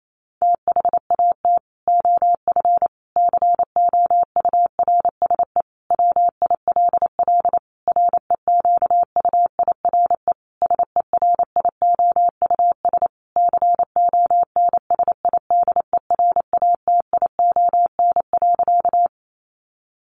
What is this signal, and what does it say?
Morse code, heard as a keyed tone, about 28 wpm, 700 Hz. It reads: THAT OF COURSE WILL REQUIRE SERIOUS CONSIDERATION.
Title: Sentences from Top 1500 Words 28wpm